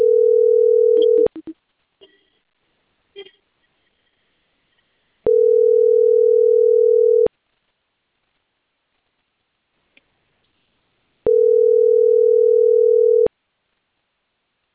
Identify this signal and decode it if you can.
Ringback tone (North America), 440 + 480 Hz.